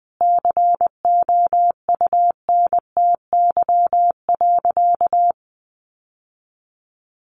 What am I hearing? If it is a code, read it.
Morse code, heard as a keyed tone, about 20 wpm, 700 Hz. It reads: COUNTY.
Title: Sentences from Top 1200 Words 20wpm